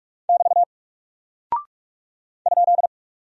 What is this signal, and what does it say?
Morse code, heard as a keyed tone, about 44 wpm, 700 Hz. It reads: <BT> ?
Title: Single Character (Letter:Number:Punctuation:Prosign) 45wpm